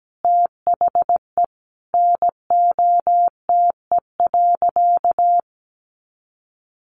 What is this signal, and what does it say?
Morse code, heard as a keyed tone, about 17 wpm, 700 Hz. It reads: THE NOTE.
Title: Sentences from Top 500 Words 17wpm